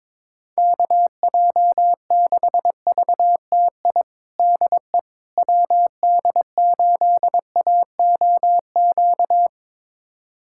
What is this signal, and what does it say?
Morse code, heard as a keyed tone, about 22 wpm, 700 Hz. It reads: KJ6VTI DE WD8AOQ
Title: USA Callsign DE Callsign - 6 Characters - 22wpm